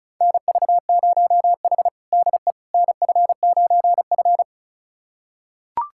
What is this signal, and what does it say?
Morse code, heard as a keyed tone, about 36 wpm, 700 Hz. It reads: NV0H DE NF9F
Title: USA Callsign DE Callsign - 4 Characters - 35wpm